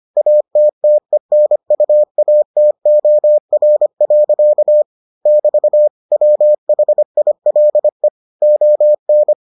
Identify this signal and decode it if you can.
Morse code, heard as a keyed tone, about 25 wpm, 600 Hz. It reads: ATTENUATOR. <BT> WHILE ON